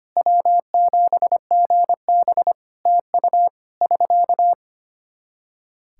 Morse code, heard as a keyed tone, about 25 wpm, 700 Hz. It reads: W7GB TU <SK>
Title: Practice QSO - Exchange 4 - Ending - 25wpm